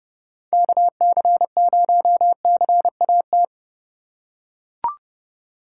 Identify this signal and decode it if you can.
Morse code, heard as a keyed tone, about 30 wpm, 700 Hz. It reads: KC0CAT